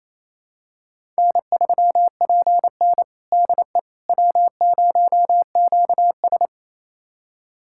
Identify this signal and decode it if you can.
Morse code, heard as a keyed tone, about 28 wpm, 700 Hz. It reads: N3PN DE W0QS
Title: USA Callsign DE Callsign - 4 Characters - 28wpm